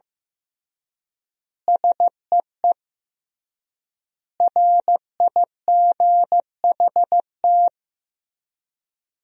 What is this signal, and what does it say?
Morse code, heard as a keyed tone, about 16 wpm, 700 Hz. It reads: SEE RIGHT
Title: 3x Word-Spacing - Sentences from Top 200 Words - 15wpm